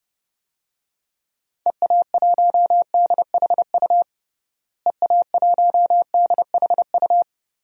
Morse code, heard as a keyed tone, about 31 wpm, 700 Hz. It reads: EA1DHU EA1DHU
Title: DXCC-100 Call Signs - 6 Characters - Repeated 3x - 30wpm